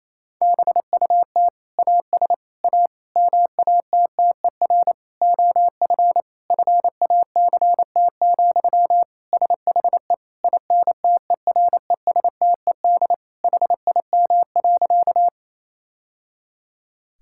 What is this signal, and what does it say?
Morse code, heard as a keyed tone, about 28 wpm, 700 Hz. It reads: BUT AS A MATTER OF FACT, SHE INTERESTED HIM.
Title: Sentences from Top 1000 Words - Review Part 6 - 28wpm